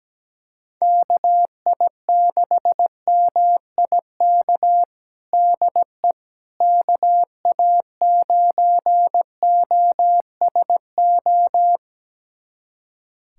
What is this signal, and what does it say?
Morse code, heard as a keyed tone, about 17 wpm, 700 Hz. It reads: KI6MIK DE KA9OSO
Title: USA Callsign DE Callsign - 6 Characters - 17wpm